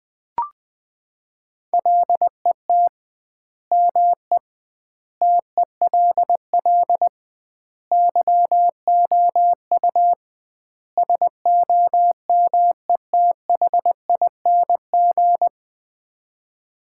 Morse code, heard as a keyed tone, about 20 wpm, 700 Hz. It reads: LET ME TELL YOU SOMETHING
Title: 2x Word-Spacing - Sets of 5 Words - 20wpm